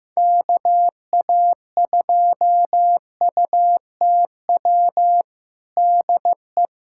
Morse code, heard as a keyed tone, about 15 wpm, 700 Hz. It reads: KA2UTW DE
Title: USA Callsign DE Callsign 15wpm